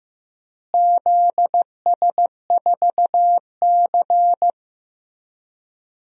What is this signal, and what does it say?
Morse code, heard as a keyed tone, about 16 wpm, 700 Hz. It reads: ZS4C
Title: DXCC-100 Call Signs - 4 Characters - Repeated 3x - 15wpm